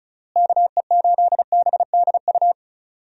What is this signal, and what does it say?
Morse code, heard as a keyed tone, about 34 wpm, 700 Hz. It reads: KE8BDU
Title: USA Call Signs - 6 Characters - Repeated 3x - 35wpm